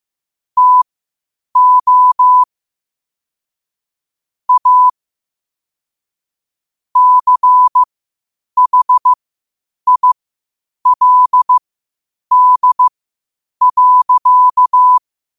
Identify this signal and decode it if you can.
Morse code, heard as a keyed tone, about 15 wpm, spaced as for 8 wpm, 1000 Hz. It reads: TO A CHILD.
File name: qotdfiles / 8wpm